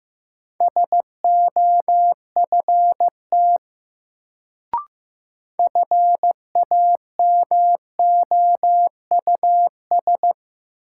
Morse code, heard as a keyed tone, about 15 wpm, 700 Hz. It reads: SOFT FAMOUS
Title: Sentences from Top 1500 Words - New Word Drill Alt - 15wpm